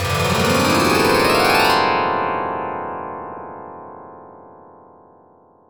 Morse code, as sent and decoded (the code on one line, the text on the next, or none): none